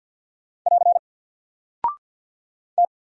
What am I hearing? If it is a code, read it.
Morse code, heard as a keyed tone, about 50 wpm, 700 Hz. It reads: <AR> T
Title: Single Character (Letter:Number:Punctuation:Prosign) 50wpm